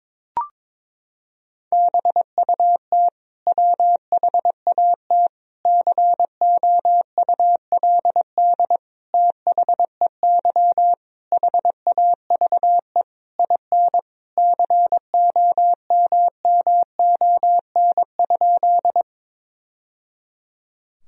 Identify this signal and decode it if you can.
Morse code, heard as a keyed tone, about 22 wpm, 700 Hz. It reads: BUT WHAT COULD THEY HAVE IN COMMON?
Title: Sentences from Top 1000 Words - Review Part 3 - 22wpm